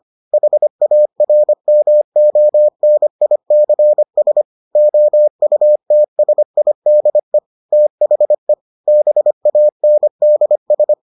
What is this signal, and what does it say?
Morse code, heard as a keyed tone, about 25 wpm, 600 Hz. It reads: HARMONICS OUTSIDE THE BANDS